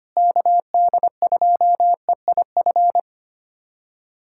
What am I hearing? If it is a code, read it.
Morse code, heard as a keyed tone, about 25 wpm, 700 Hz. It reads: KD2EIF
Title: USA Callsign DE Callsign - 6 Characters - 25wpm